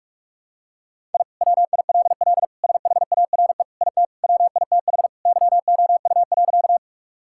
Morse code, heard as a keyed tone, about 42 wpm, 700 Hz. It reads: I WILL SHARE IT WITH YOU.
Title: Sentences from Top 1000 Words - Review Part 6 - 45wpm